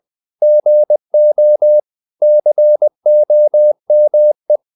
Morse code, heard as a keyed tone, about 20 wpm, 600 Hz. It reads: GO COME
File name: CW / Practice / 500MostCommonEnglishWords20